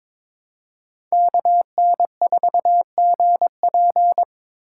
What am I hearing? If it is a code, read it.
Morse code, heard as a keyed tone, about 22 wpm, 700 Hz. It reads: KN4GP
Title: USA Call Signs - 3x Character Spacing - 22wpm